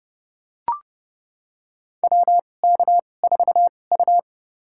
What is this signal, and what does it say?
Morse code, heard as a keyed tone, about 30 wpm, spaced as for 22 wpm, 700 Hz. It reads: WK4U
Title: USA Call Signs - 2x Character Spacing - 30wpm